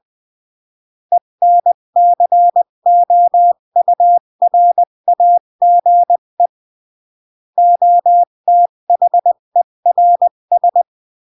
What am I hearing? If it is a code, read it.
Morse code, heard as a keyed tone, about 20 wpm, 700 Hz. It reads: ENCOURAGE OTHERS
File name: CW / jodorowsky / 82 commandments 20-15